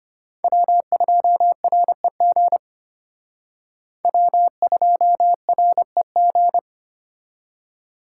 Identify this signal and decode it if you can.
Morse code, heard as a keyed tone, about 32 wpm, 700 Hz. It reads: W2REG W2REG